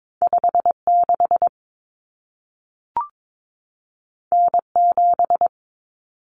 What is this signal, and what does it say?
Morse code, heard as a keyed tone, about 22 wpm, 700 Hz. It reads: 56 N7
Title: Common Cut Numbers - 2-Digit 22wpm